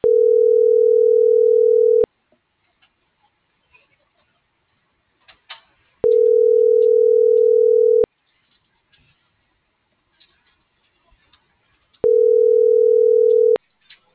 Ringback tone (North America), 440 + 480 Hz.